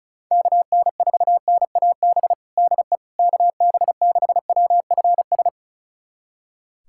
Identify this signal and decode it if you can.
Morse code, heard as a keyed tone, about 36 wpm, 700 Hz. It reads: KN4NAB DE KB6WFS
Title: USA Callsign DE Callsign - 6 Characters - 35wpm